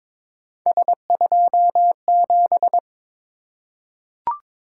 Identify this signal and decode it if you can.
Morse code, heard as a keyed tone, about 22 wpm, 700 Hz. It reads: S27
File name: practice / 22wpm / Three Letter-Number Combinations - 22wpm